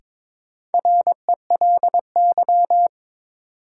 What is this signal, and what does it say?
Morse code, heard as a keyed tone, about 22 wpm, 700 Hz. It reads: RELY
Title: Sentences from Top 1200 Words - New Word Drill Alt - 22wpm